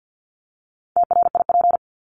Morse code, heard as a keyed tone, about 50 wpm, 700 Hz. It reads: TRIP
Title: Top 1251-1500 Words 50wpm